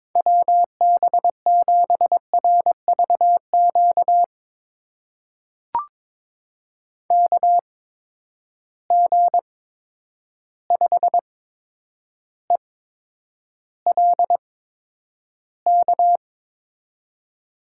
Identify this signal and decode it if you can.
Morse code, heard as a keyed tone, about 22 wpm, spaced as for 9 wpm, 700 Hz. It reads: WB7RVQ KG5ELK
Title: USA Call Signs - 6 Characters - 8x Character Spacing - 22wpm